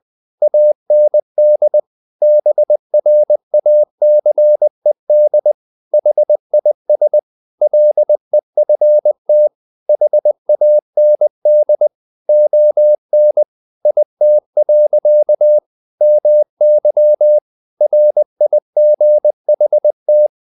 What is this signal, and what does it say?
Morse code, heard as a keyed tone, about 20 wpm, 600 Hz. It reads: AND BRACED HIS LEFT HAND ON IT. MY RIGHT
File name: the old man and the sea part 2 20wpm